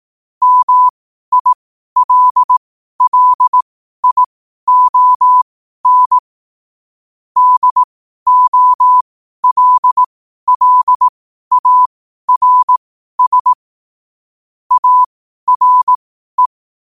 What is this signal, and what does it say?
Morse code, heard as a keyed tone, about 18 wpm, spaced as for 12 wpm, 1000 Hz. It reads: MILLION DOLLARS ARE
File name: qotdfiles / 12wpm